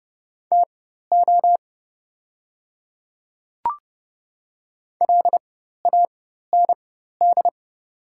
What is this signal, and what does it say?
Morse code, heard as a keyed tone, about 30 wpm, spaced as for 14 wpm, 700 Hz. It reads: TO LAND
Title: 4x Character Spacing - Top 200 Words - 30wpm